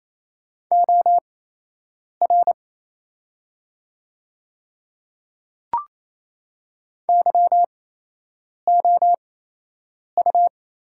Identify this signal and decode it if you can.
Morse code, heard as a keyed tone, about 27 wpm, spaced as for 8 wpm, 700 Hz. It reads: OR YOU
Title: 8x Character Spacing - Top 200 Words - 28wpm